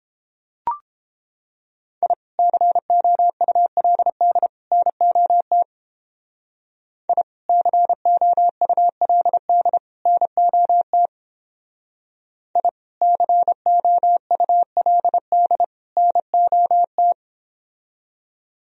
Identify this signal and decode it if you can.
Morse code, heard as a keyed tone, about 29 wpm, 700 Hz. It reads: I COULD NOT I COULD NOT I COULD NOT